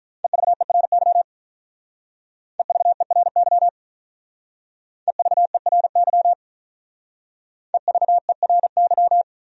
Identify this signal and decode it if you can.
Morse code, heard as a keyed tone, about 43 wpm, 700 Hz. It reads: EVERY EVERY EVERY EVERY
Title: Speed-Racing Top 200 Words 35wpm